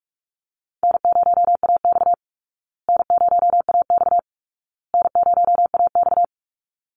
Morse code, heard as a keyed tone, about 44 wpm, 700 Hz. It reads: N0AX N0AX N0AX
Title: Call Signs Repeated 2x - 45wpm